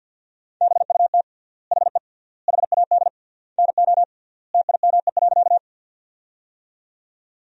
Morse code, heard as a keyed tone, about 50 wpm, 700 Hz. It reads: BUT HE HAD NO TIME.
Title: 3x Word-Spacing - Sentences from Top 100 Words - 50wpm